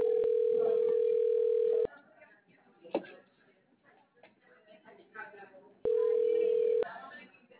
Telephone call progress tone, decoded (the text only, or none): ringback tone